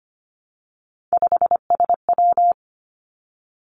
Morse code, heard as a keyed tone, about 25 wpm, 700 Hz. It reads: 5SW
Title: Three Challenging Letter-Number Combinations 25wpm